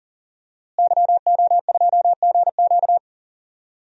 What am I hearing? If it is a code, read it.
Morse code, heard as a keyed tone, about 37 wpm, 700 Hz. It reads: YO2GQ
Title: DXCC-100 Call Signs - 5 Characters - Repeated 2x - 40wpm